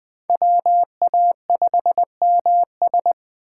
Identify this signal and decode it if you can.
Morse code, heard as a keyed tone, about 20 wpm, 700 Hz. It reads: WA5MS
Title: Speed Racing - USA 2x2 Call Signs 17wpm